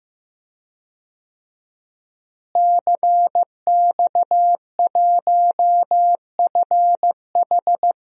Morse code, heard as a keyed tone, about 15 wpm, 700 Hz. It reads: CX1FH
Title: DXCC-100 Call Signs - 5 Characters - Repeated 2x - 15wpm